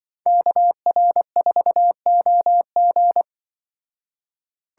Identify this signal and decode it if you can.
Morse code, heard as a keyed tone, about 23 wpm, 700 Hz. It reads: KR4OG
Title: Speed Racing - USA Call Signs 20wpm